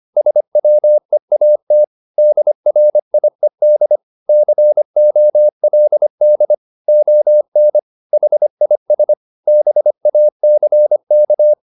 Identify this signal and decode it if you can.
Morse code, heard as a keyed tone, about 25 wpm, 600 Hz. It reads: SWEAT DRIED COLD ON HIS BACK
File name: the old man and the sea part 1 25wpm